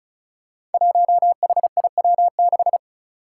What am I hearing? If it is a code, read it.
Morse code, heard as a keyed tone, about 36 wpm, 700 Hz. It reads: 1HIW6
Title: Five Challenging Letter-Number Combinations 35wpm